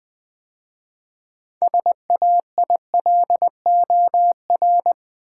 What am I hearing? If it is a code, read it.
Morse code, heard as a keyed tone, about 20 wpm, 700 Hz. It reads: SAILOR
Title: 6 Letter Words - First 3 Letters Given - 20wpm